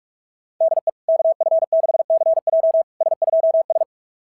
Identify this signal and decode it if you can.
Morse code, heard as a keyed tone, about 44 wpm, 690 Hz. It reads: DE KF6CJ S2S
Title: SOTA QSO Practice 45wpm